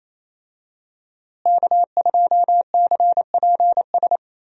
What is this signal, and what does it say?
Morse code, heard as a keyed tone, about 27 wpm, 700 Hz. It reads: K2CPS
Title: USA Call Signs - 5 Characters - Repeated 3x - 28wpm